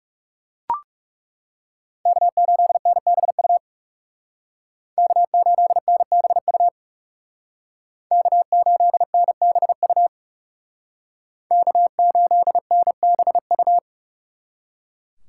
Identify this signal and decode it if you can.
Morse code, heard as a keyed tone, about 34 wpm, 700 Hz. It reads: K8NBU K8NBU K8NBU K8NBU
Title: Speed Racing - USA 1x3 Call Signs 30wpm